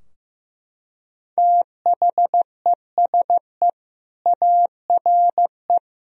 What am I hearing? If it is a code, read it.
Morse code, heard as a keyed tone, about 16 wpm, 700 Hz. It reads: THESE ARE
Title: Sentences from Top 1000 Words - Review Part 3 - 15wpm